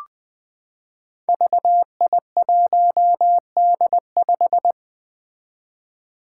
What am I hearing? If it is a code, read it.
Morse code, heard as a keyed tone, about 20 wpm, 700 Hz. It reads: VI1D5
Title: Five Challenging Letter-Number Combinations 20wpm